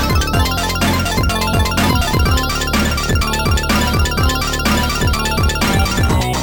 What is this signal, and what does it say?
There is no code.